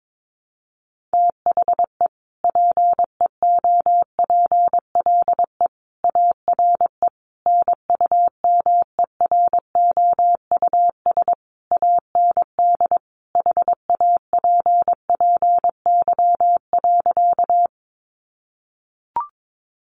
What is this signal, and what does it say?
Morse code, heard as a keyed tone, about 22 wpm, 700 Hz. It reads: THE PEOPLE ARE NUMEROUS AND HAPPY.
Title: Sentences from Top 1400 Words 22wpm